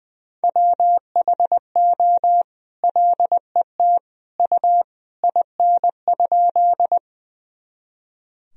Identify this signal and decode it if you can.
Morse code, heard as a keyed tone, about 20 wpm, 700 Hz. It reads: WHO LET U IN?
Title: Sentences from Top 300 Words and Abbreviations 20wpm